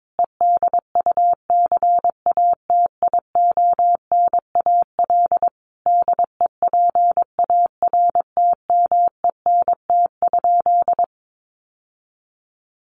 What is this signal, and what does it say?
Morse code, heard as a keyed tone, about 22 wpm, 700 Hz. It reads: EDUCATIONAL DEPARTMENT?
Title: Sentences from Top 1200 Words 22wpm